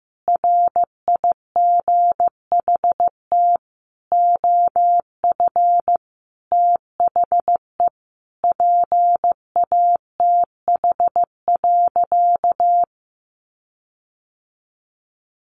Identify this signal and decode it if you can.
Morse code, heard as a keyed tone, about 15 wpm, 700 Hz. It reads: RIGHT OF THE PATH.
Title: Sentences from Top 1400 Words 15wpm